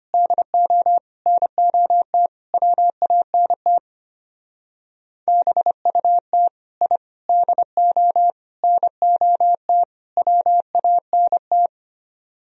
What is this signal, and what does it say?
Morse code, heard as a keyed tone, about 32 wpm, 700 Hz. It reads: DO NOT WANT BUT I DO NOT WANT